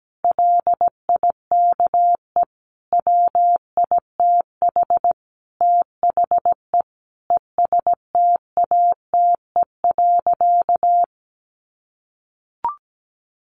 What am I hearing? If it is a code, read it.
Morse code, heard as a keyed tone, about 17 wpm, 700 Hz. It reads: LIKE WITH THE ESTATE.
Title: Sentences from Top 1200 Words 17wpm